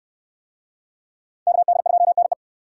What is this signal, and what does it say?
Morse code, heard as a keyed tone, about 69 wpm, 700 Hz. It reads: KD2NE